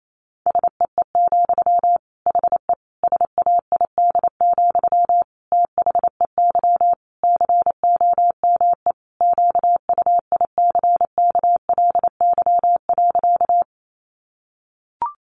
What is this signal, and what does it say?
Morse code, heard as a keyed tone, about 28 wpm, 700 Hz. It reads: SEE, HE SAID, THEY COME QUICKLY.
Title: Sentences from Top 700 Words 28wpm